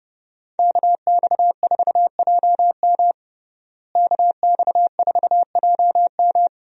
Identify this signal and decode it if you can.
Morse code, heard as a keyed tone, about 30 wpm, 700 Hz. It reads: KX4JM KX4JM
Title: USA Call Signs - 5 Characters - Repeated 3x - 30wpm